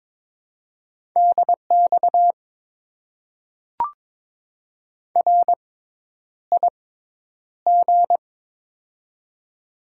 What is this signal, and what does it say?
Morse code, heard as a keyed tone, about 22 wpm, spaced as for 10 wpm, 700 Hz. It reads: DX RIG
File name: Top 90 QSO Elements - 6x Character Spacing - 22wpm